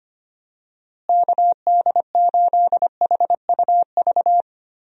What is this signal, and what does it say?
Morse code, heard as a keyed tone, about 25 wpm, 700 Hz. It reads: KD8HUV
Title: USA Call Signs - 6 Characters - Repeated 2x - 25wpm